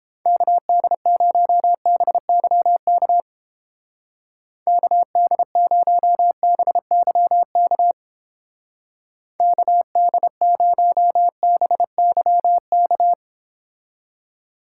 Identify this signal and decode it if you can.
Morse code, heard as a keyed tone, about 32 wpm, 700 Hz. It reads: KD0BYK KD0BYK KD0BYK